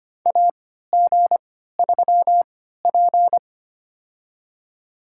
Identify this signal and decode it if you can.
Morse code, heard as a keyed tone, about 25 wpm, spaced as for 14 wpm, 700 Hz. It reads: AG3P
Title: USA Call Signs - 4 Characters - 3x Character Spacing - 25wpm